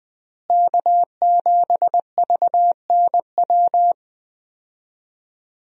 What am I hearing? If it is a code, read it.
Morse code, heard as a keyed tone, about 21 wpm, 700 Hz. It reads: K7VNW
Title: USA Call Signs - 5 Characters - Repeated 2x - 20wpm